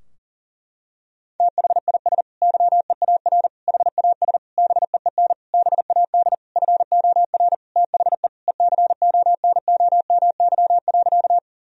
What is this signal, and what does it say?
Morse code, heard as a keyed tone, about 40 wpm, 700 Hz. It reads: THIS YEAR HAS BEEN BAD FOR THE ECONOMY.
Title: Sentences from Top 1000 Words - Review Part 3 - 40wpm